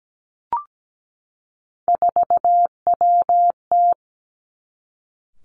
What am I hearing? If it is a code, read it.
Morse code, heard as a keyed tone, about 17 wpm, 700 Hz. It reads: 4WT
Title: Three Challenging Letter-Number Combinations 17wpm